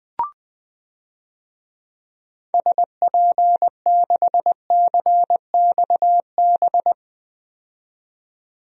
Morse code, heard as a keyed tone, about 20 wpm, 700 Hz. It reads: SP6CXB
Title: DXCC-100 Call Signs - 6 Characters - 20wpm